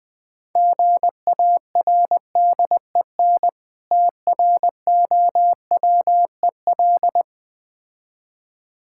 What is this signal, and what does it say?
Morse code, heard as a keyed tone, about 20 wpm, 700 Hz. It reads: GARDEN TROWEL
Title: Difficult Words - Speed-Racing with Hints - 20wpm